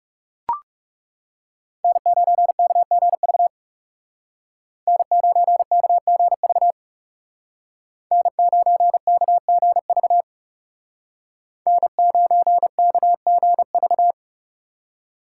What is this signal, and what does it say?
Morse code, heard as a keyed tone, about 36 wpm, 700 Hz. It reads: N9KGV N9KGV N9KGV N9KGV
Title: Speed Racing - USA 1x3 Call Signs 30wpm